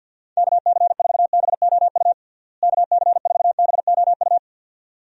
Morse code, heard as a keyed tone, about 50 wpm, 700 Hz. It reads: KC4BOU KC4BOU
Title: USA Call Signs - 6 Characters - Repeated 2x - 50wpm